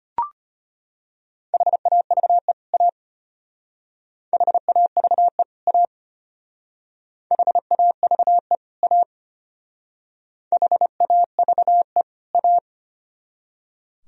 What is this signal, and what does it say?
Morse code, heard as a keyed tone, about 32 wpm, 700 Hz. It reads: HAVE A HAVE A HAVE A HAVE A